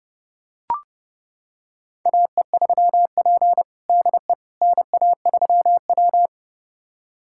Morse code, heard as a keyed tone, about 30 wpm, 700 Hz. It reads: AE3P DE NA3W
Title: USA Callsign DE Callsign - 4 Characters - 30wpm